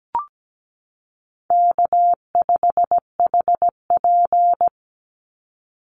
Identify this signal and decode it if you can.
Morse code, heard as a keyed tone, about 17 wpm, 700 Hz. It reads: K5HP
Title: Easy Call Signs - 17wpm